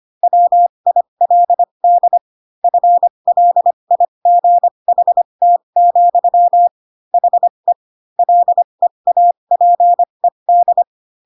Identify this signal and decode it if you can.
Morse code, heard as a keyed tone, about 25 wpm, spaced as for 22 wpm, 700 Hz. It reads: WILD FLIGHT, HE LEAPED
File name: Pinocchio / Pinocchio ch4 18 25 ik1hge